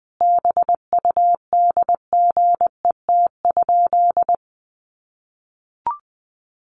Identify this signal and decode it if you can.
Morse code, heard as a keyed tone, about 20 wpm, 700 Hz. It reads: BUDGET?
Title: Sentences from Top 1200 Words 20wpm